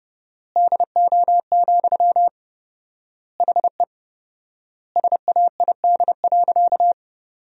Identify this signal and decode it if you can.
Morse code, heard as a keyed tone, about 30 wpm, 700 Hz. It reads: DO, HE SAID.
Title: 4x Word-Spacing - Sentences from Top 100 Words - 30wpm